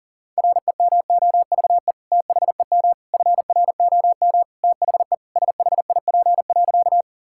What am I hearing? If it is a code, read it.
Morse code, heard as a keyed tone, about 37 wpm, 700 Hz. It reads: REMOVE THEM FROM THE SHIP.
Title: Sentences from Top 1000 Words - Review Part 5 - 40wpm